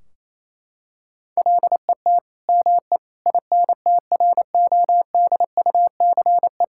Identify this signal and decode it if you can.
Morse code, heard as a keyed tone, about 28 wpm, 700 Hz. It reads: LET ME INTRODUCE
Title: Sentences from Top 1600 Words 28wpm